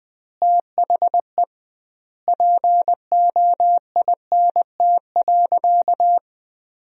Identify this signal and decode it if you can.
Morse code, heard as a keyed tone, about 20 wpm, 700 Hz. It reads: THE POINT.
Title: 2x Word-Spacing - Sentences from Top 200 Words - 20wpm